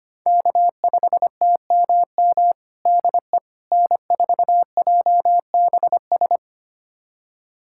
Morse code, heard as a keyed tone, about 25 wpm, 700 Hz. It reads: K5TMM DE N4JBS
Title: USA Callsign DE Callsign - 5 Characters - 25wpm